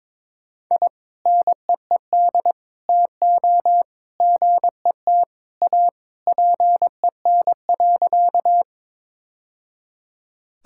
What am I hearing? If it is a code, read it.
Morse code, heard as a keyed tone, about 22 wpm, 700 Hz. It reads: I NEED TO GET A PEN.
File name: practice / 22wpm / Sentences with 2-Word Phrases - 22wpm